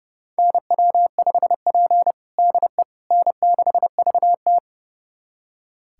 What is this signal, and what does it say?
Morse code, heard as a keyed tone, about 30 wpm, 700 Hz. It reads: NW5P DE N6VT
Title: USA Callsign DE Callsign - 4 Characters - 30wpm